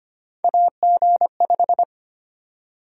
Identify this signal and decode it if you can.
Morse code, heard as a keyed tone, about 25 wpm, 700 Hz. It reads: AG5
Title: USA Call Sign Prefixes - Repeated 2x - 25wpm